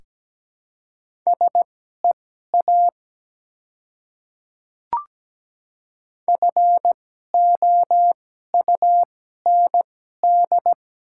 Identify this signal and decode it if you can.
Morse code, heard as a keyed tone, about 17 wpm, spaced as for 12 wpm, 700 Hz. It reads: SEA FOUND
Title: 2x Character Spacing - Top 300 Words - 17wpm